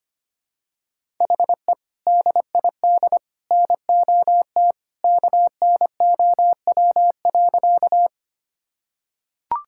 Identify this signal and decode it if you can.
Morse code, heard as a keyed tone, about 25 wpm, 700 Hz. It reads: HE DID NOT KNOW.
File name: practice / 25wpm / Sentences with 2-Word Phrases - 25wpm